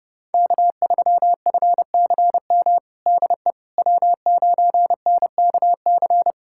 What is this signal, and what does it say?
Morse code, heard as a keyed tone, about 30 wpm, 700 Hz. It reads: K3FCM DE W9NKC
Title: USA Callsign DE Callsign - 5 Characters - 30wpm